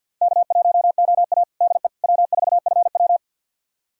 Morse code, heard as a keyed tone, about 50 wpm, 700 Hz. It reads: K1OA DE W4FW
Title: USA Callsign DE Callsign - 4 Characters - 50wpm